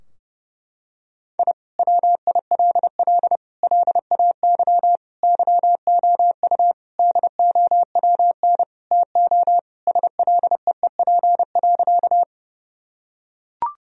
Morse code, heard as a keyed tone, about 30 wpm, 700 Hz. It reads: I WILL LAY YOU DOWN TO SLEEP.
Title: Sentences from Top 1000 Words - Review Part 2 - 30wpm